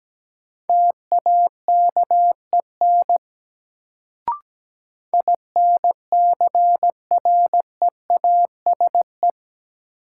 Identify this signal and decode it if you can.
Morse code, heard as a keyed tone, about 17 wpm, 700 Hz. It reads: TAKEN INCREASE
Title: Sentences from Top 1000 Words - Review Part 4 - New Word Drill - 17wpm